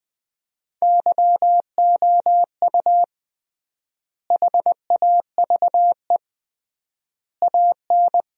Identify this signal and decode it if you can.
Morse code, heard as a keyed tone, about 20 wpm, 700 Hz. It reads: YOU HAVE AN
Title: 3x Word-Spacing - Sentences from Top 400 Words - 20wpm